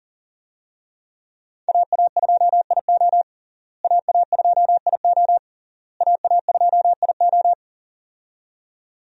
Morse code, heard as a keyed tone, about 37 wpm, 700 Hz. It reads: AA2IO AA2IO AA2IO
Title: USA Call Signs - 5 Characters - Repeated 3x - 40wpm